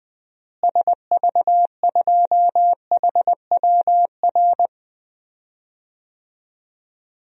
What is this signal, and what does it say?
Morse code, heard as a keyed tone, about 20 wpm, 700 Hz. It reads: SV2HWR
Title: DXCC-100 Call Signs - 6 Characters - Repeated 2x - 20wpm